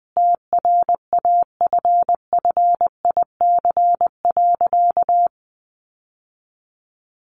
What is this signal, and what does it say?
Morse code, heard as a keyed tone, about 20 wpm, 700 Hz. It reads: TRAFFIC.
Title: Sentences from Top 1300 Words 20wpm